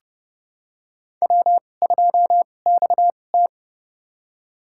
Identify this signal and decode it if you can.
Morse code, heard as a keyed tone, about 32 wpm, spaced as for 22 wpm, 700 Hz. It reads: W2XT
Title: USA Call Signs - 2x Character Spacing - 30wpm